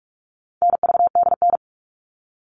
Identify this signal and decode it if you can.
Morse code, heard as a keyed tone, about 44 wpm, 700 Hz. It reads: NVDN
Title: All Cut Numbers - 4-Digit 45wpm